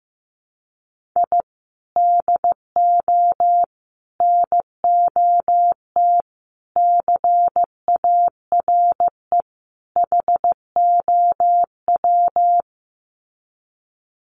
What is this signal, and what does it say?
Morse code, heard as a keyed tone, about 15 wpm, 700 Hz. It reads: I DO NOT CARE HOW
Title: Sets of 5 Words Encore2 15wpm